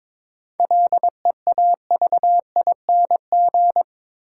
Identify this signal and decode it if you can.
Morse code, heard as a keyed tone, about 22 wpm, 700 Hz. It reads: LEAVING
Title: Sentences from Top 1000 Words - Review Part 4 - New Word Drill - 22wpm